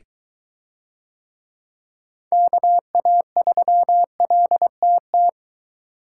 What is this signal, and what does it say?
Morse code, heard as a keyed tone, about 21 wpm, 700 Hz. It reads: KA3LTT